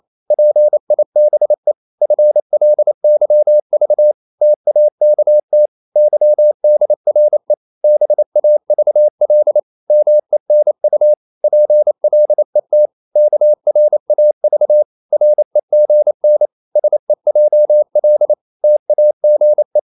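Morse code, heard as a keyed tone, about 28 wpm, 600 Hz. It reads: PIBE FLYV TAKT YDRE BAVL MENU PLET KRAV REGN SEJL TAGE
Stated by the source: Danish short words sent in Morse at 28 wpm, from CW med Gnister.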